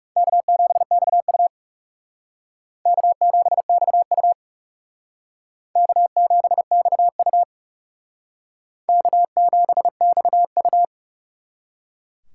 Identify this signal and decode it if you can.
Morse code, heard as a keyed tone, about 40 wpm, 700 Hz. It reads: K7XU K7XU K7XU K7XU